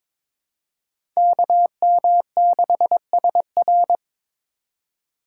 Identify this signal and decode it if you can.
Morse code, heard as a keyed tone, about 22 wpm, 700 Hz. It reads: KM6SR
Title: USA Call Signs - 5 Characters - 2x Character Spacing - 22wpm